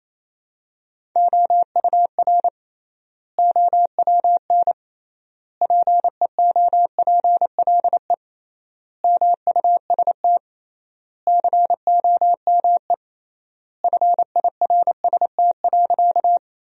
Morse code, heard as a keyed tone, about 27 wpm, 700 Hz. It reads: OUR OWN PEOPLE MUST COME FIRST.
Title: 3x Word-Spacing - Sentences from Top 300 Words - 28wpm